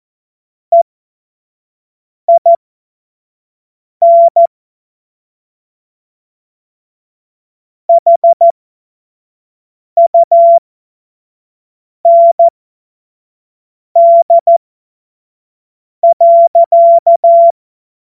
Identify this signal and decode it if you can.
Morse code, heard as a keyed tone, about 14 wpm, spaced as for 5 wpm, 680 Hz. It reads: EIN HUND.
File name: Morsekurs / Witze / Witz 20. Juni 2023 25 F70